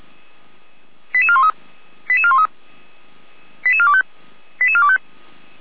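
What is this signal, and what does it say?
Five-tone selective call (ZVEI), heard as quick runs of five tones, tones of about 70 ms. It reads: 88413 88413 88425 88425
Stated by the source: ZVEI five-tone selective call 88251 received on SDR.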